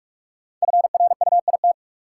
Morse code, heard as a keyed tone, about 46 wpm, 700 Hz. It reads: FRUIT
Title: Sentences from Top 1300 Words - New Word Drill - 45wpm